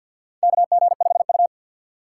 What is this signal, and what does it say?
Morse code, heard as a keyed tone, about 52 wpm, 700 Hz. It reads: KG5U
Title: USA Call Signs - 4 Characters - 2x Character Spacing - 50wpm